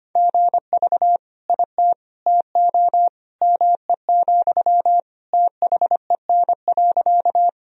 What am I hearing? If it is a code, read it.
Morse code, heard as a keyed tone, about 25 wpm, 700 Hz. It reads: GV IT TO ME, THEN.
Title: Sentences from Top 200 Words and Abbreviations 25wpm